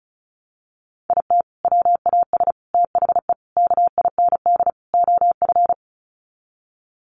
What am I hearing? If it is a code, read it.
Morse code, heard as a keyed tone, about 34 wpm, 700 Hz. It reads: IT WAS THE KIND OF
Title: Sets of 5 Words Encore2 35wpm